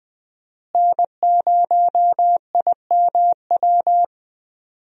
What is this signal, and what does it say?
Morse code, heard as a keyed tone, about 20 wpm, 700 Hz. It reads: N0IMW
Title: USA Call Signs - 5 Characters - Repeated 2x - 20wpm